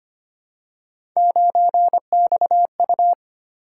Morse code, heard as a keyed tone, about 25 wpm, 700 Hz. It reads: 9XU
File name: Three Letter-Number Combinations - 25wpm